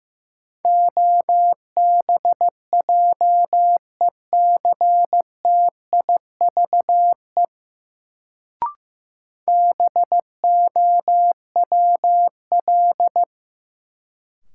Morse code, heard as a keyed tone, about 15 wpm, 700 Hz. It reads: OBJECTIVE BOWL
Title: Sentences from Top 1200 Words - New Word Drill Alt - 15wpm